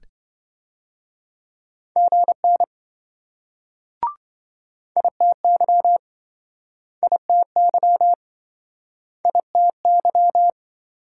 Morse code, heard as a keyed tone, about 27 wpm, 700 Hz. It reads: GN ITY ITY ITY